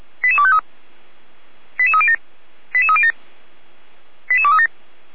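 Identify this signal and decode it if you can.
Five-tone selective call (ZVEI), heard as quick runs of five tones, tones of about 70 ms. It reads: 88253 88307 88307 88237
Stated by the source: ZVEI five-tone call received by SDR, archived by Feuerwehr Orlamünde.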